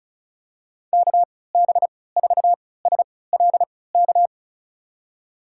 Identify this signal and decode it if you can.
Morse code, heard as a keyed tone, about 36 wpm, spaced as for 20 wpm, 700 Hz. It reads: KB4SLK
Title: USA Call Signs - 6 Characters - 3x Character Spacing - 35wpm